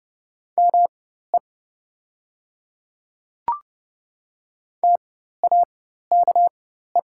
Morse code, heard as a keyed tone, about 30 wpm, spaced as for 14 wpm, 700 Hz. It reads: ME TAKE
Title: 4x Character Spacing - Top 500 Words - 30wpm